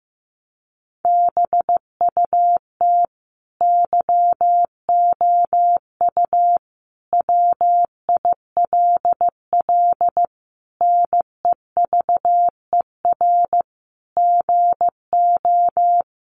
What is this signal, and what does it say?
Morse code, heard as a keyed tone, about 15 wpm, 700 Hz. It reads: BUT YOU WILL NEVER GO
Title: Sentences from Top 1500 Words 15wpm